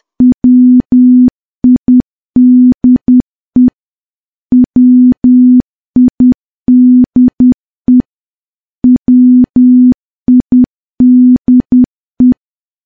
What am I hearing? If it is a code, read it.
Morse code, heard as a keyed tone, about 10 wpm, 250 Hz. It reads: WIDE WIDE WIDE
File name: morse / morse-training-10WPM